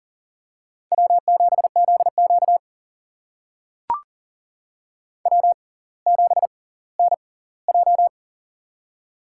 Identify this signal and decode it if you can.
Morse code, heard as a keyed tone, about 40 wpm, spaced as for 20 wpm, 700 Hz. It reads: W7ZQ W7NJ
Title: USA Call Signs - 6x Character Spacing - 40wpm